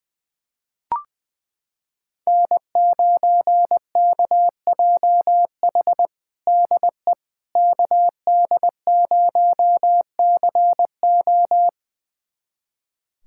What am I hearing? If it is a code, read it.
Morse code, heard as a keyed tone, about 20 wpm, 700 Hz. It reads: N9KJH DE KD0CO
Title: USA Callsign DE Callsign - 5 Characters - 20wpm